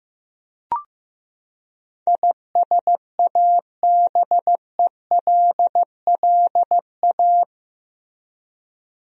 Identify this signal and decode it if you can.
Morse code, heard as a keyed tone, about 15 wpm, 700 Hz. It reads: ISABELLA
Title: USA Common Names - 15wpm